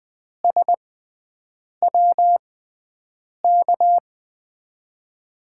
Morse code, heard as a keyed tone, about 21 wpm, spaced as for 7 wpm, 700 Hz. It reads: SWK
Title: USA Call Signs - 5 Characters - 6x Character Spacing - 20wpm